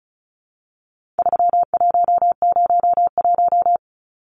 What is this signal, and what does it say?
Morse code, heard as a keyed tone, about 34 wpm, 700 Hz. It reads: 3101
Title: Numbers - 4-Digit 35wpm